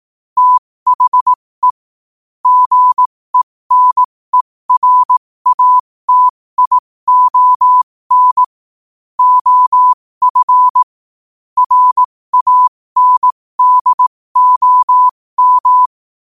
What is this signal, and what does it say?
Morse code, heard as a keyed tone, about 18 wpm, spaced as for 15 wpm, 1000 Hz. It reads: THE GENERATION OF RANDOM
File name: qotdfiles / 15wpm